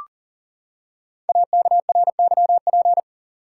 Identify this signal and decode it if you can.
Morse code, heard as a keyed tone, about 42 wpm, 690 Hz. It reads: AKRYP
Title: Code Groups - 5 Letters - 40wpm